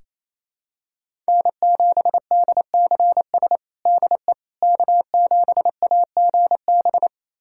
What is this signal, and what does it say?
Morse code, heard as a keyed tone, about 28 wpm, 700 Hz. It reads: N7DCS DE K7AGB
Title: USA Callsign DE Callsign 28wpm